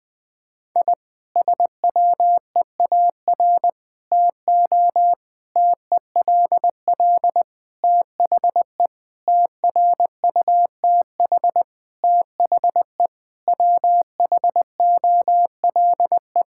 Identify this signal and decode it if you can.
Morse code, heard as a keyed tone, about 20 wpm, 700 Hz. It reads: I SWEAR TO TELL THE TRUTH THE WHOLE
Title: Sentences with 9-Word Phrases - 20wpm